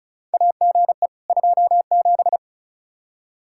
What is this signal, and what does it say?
Morse code, heard as a keyed tone, about 36 wpm, 700 Hz. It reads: AGE 27
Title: K1USN SST QSO Practice 35wpm